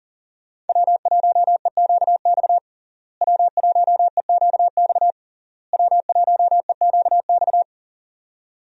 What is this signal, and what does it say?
Morse code, heard as a keyed tone, about 40 wpm, 700 Hz. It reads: W1EQX W1EQX W1EQX
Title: USA Call Signs - 5 Characters - Repeated 3x - 40wpm